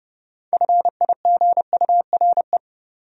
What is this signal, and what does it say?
Morse code, heard as a keyed tone, about 36 wpm, 700 Hz. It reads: FIGURE
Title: Speed-Racing Top 500 Words 30wpm